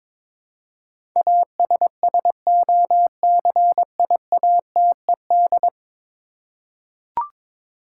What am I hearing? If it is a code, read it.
Morse code, heard as a keyed tone, about 22 wpm, 700 Hz. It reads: ASSOCIATED
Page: morse-code-ninja-practice-22wpm